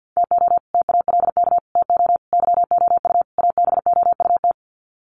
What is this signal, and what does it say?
Morse code, heard as a keyed tone, about 50 wpm, 700 Hz. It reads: TO TALK TO YOU ABOUT
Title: Sets of 5 Words Encore 50wpm